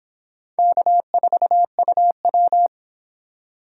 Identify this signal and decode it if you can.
Morse code, heard as a keyed tone, about 26 wpm, 700 Hz. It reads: K4UW